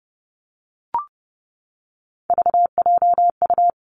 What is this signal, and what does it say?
Morse code, heard as a keyed tone, about 30 wpm, 700 Hz. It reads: VJU
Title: Three Challenging Letter-Number Combinations 30wpm